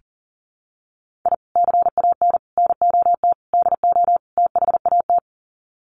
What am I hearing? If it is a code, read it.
Morse code, heard as a keyed tone, about 40 wpm, 700 Hz. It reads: I CAN NOT DO THAT
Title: Sets of 5 Words Encore2 40wpm